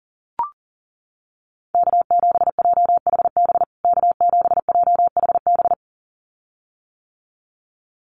Morse code, heard as a keyed tone, about 40 wpm, 700 Hz. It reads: K7JHB K7JHB
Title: USA Call Signs Repeated 2x 40wpm